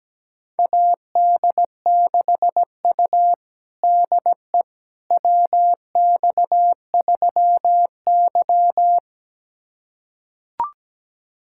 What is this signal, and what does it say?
Morse code, heard as a keyed tone, about 17 wpm, 700 Hz. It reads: AD6U DE WX3Y
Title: USA Callsign DE Callsign - 4 Characters - 17wpm